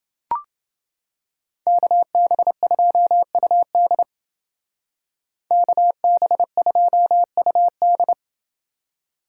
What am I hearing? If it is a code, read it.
Morse code, heard as a keyed tone, about 28 wpm, 700 Hz. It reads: KB2UD KB2UD